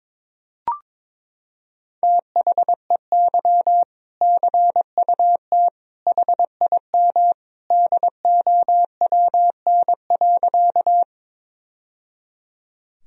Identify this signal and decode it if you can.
Morse code, heard as a keyed tone, about 22 wpm, 700 Hz. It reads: THEY CUT HIM DOWN.
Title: Sentences from Top 1000 Words - Review Part 1 - 22wpm